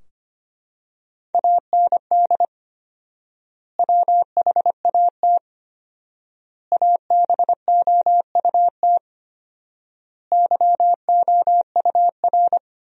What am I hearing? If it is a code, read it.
Morse code, heard as a keyed tone, about 25 wpm, 700 Hz. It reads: AND WHAT ABOUT YOUR
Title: 4x Word-Spacing - Sentences from Top 200 Words - 25wpm